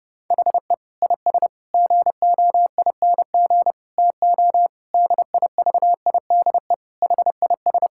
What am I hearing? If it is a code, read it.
Morse code, heard as a keyed tone, about 32 wpm, 700 Hz. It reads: HE IS GOING TO DIVIDE HIS
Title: Sentences from Top 1600 Words 30wpm